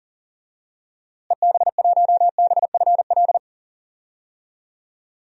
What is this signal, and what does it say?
Morse code, heard as a keyed tone, about 40 wpm, 700 Hz. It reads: EB1BFL
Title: DXCC-100 Call Signs - 6 Characters - Repeated 2x - 40wpm